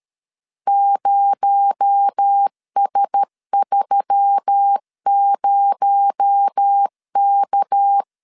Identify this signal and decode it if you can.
Morse code, heard as a keyed tone, about 13 wpm, 800 Hz. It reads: 0S30K
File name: LEZIONI / 6 PKX0 / 6 PKX0 12wpm 2231